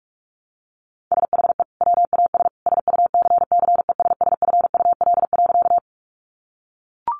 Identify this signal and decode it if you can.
Morse code, heard as a keyed tone, about 44 wpm, 700 Hz. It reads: SHE WAS SUCCESSFUL.
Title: Sentences from Top 900 Words 45wpm